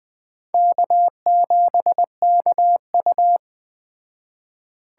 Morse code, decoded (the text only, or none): K7KU